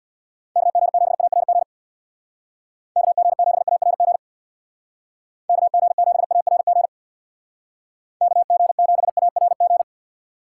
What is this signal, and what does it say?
Morse code, heard as a keyed tone, about 64 wpm, 700 Hz. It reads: KG7ARG KG7ARG KG7ARG KG7ARG